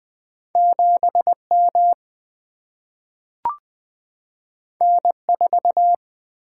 Morse code, heard as a keyed tone, about 20 wpm, 700 Hz. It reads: 7 M N 4
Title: Single-Letter Single-Number 20wpm